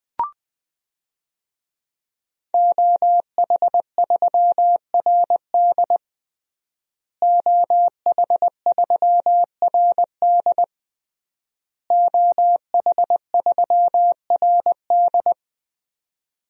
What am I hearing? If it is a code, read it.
Morse code, heard as a keyed tone, about 20 wpm, 700 Hz. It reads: OH3RD OH3RD OH3RD
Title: DXCC-100 Call Signs - 5 Characters - Repeated 3x - 20wpm